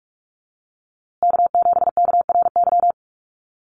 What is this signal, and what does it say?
Morse code, heard as a keyed tone, about 44 wpm, 700 Hz. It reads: K7KRY
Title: USA Call Signs Repeated 2x 45wpm